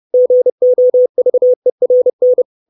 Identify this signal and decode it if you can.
Morse code, heard as a keyed tone, about 30 wpm, 500 Hz. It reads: GOVERN
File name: 632 Words 30WPM